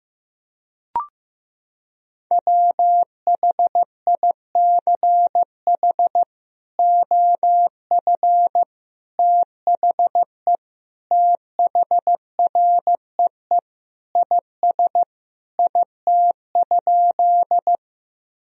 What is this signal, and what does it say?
Morse code, heard as a keyed tone, about 15 wpm, 700 Hz. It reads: WHICH OF THE THREE IS IT?
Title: Sentences from Top 1000 Words - Review Part 1 - 15wpm